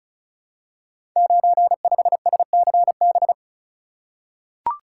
Morse code, heard as a keyed tone, about 35 wpm, 700 Hz. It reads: 95SCB
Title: Code Groups - 5 Letter:Numbers - 35wpm